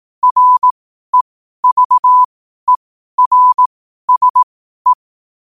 Morse code, heard as a keyed tone, about 18 wpm, spaced as for 12 wpm, 1000 Hz. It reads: REVERSE
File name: qotdfiles / 12wpm